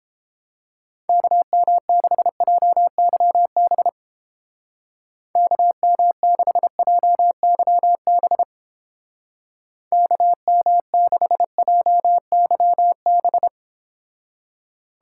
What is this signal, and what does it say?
Morse code, heard as a keyed tone, about 29 wpm, 700 Hz. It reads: KM6JYB KM6JYB KM6JYB